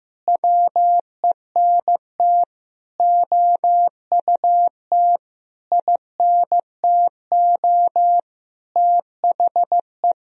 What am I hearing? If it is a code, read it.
Morse code, heard as a keyed tone, about 15 wpm, 700 Hz. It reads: WENT OUT INTO THE
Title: Sentences from Top 1000 Words - Review Part 2 - 15wpm